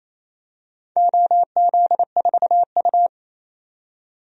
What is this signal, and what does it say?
Morse code, heard as a keyed tone, about 28 wpm, 700 Hz. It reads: OZ4U
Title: DXCC-100 Call Signs - 4 Characters - 28wpm